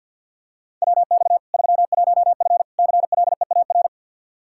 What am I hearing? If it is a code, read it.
Morse code, heard as a keyed tone, about 49 wpm, 700 Hz. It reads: WX 31F CLEAR
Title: Practice QSO - Exchange 2 - Info - 50wpm